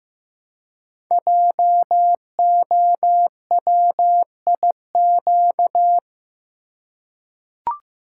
Morse code, heard as a keyed tone, about 15 wpm, 700 Hz. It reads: JOWIQ
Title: Code Groups - 5 Letters - 15wpm